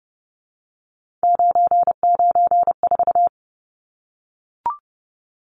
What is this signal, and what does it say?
Morse code, heard as a keyed tone, about 30 wpm, 700 Hz. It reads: 994
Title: Numbers - 3-Digit 30wpm